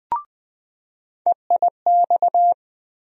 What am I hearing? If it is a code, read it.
Morse code, heard as a keyed tone, about 20 wpm, 700 Hz. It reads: EIX
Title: USA Call Sign Postfixes - 3 Letters 20wpm